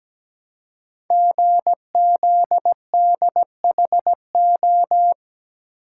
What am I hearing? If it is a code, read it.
Morse code, heard as a keyed tone, about 17 wpm, 700 Hz. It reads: GZDHO
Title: Code Groups - 5 Letters - 17wpm